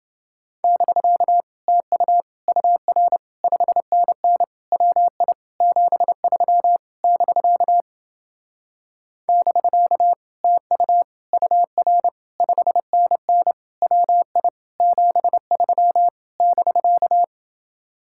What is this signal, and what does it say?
Morse code, heard as a keyed tone, about 30 wpm, 700 Hz. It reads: <BK> TU UR 5NN WI 73 <BK> <BK> TU UR 5NN WI 73 <BK>